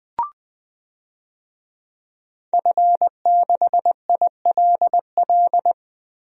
Morse code, heard as a keyed tone, about 20 wpm, 700 Hz. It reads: F6ILL
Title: DXCC-100 Call Signs - 5 Characters - Repeated 2x - 20wpm